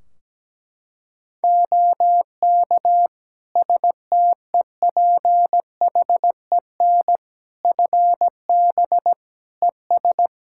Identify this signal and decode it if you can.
Morse code, heard as a keyed tone, about 17 wpm, 700 Hz. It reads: OK STEPHEN FB ES
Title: K1USN SST QSO Practice 17wpm